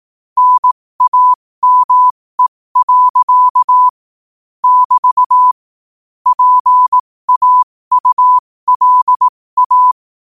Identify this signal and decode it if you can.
Morse code, heard as a keyed tone, about 18 wpm, spaced as for 15 wpm, 1000 Hz. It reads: NAME. <BT> PAULA